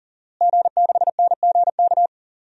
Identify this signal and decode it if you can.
Morse code, heard as a keyed tone, about 40 wpm, 700 Hz. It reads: G6NGK
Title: DXCC-100 Call Signs - 5 Characters - Repeated 3x - 40wpm